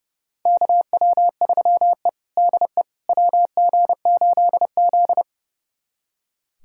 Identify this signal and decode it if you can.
Morse code, heard as a keyed tone, about 30 wpm, 700 Hz. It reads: KW3E DE WG8Z
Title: USA Callsign DE Callsign - 4 Characters - 30wpm